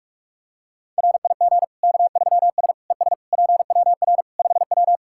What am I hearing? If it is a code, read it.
Morse code, heard as a keyed tone, about 43 wpm, 700 Hz. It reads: RIG K3S ES PWR 5W
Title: K1USN SST QSO Practice 45wpm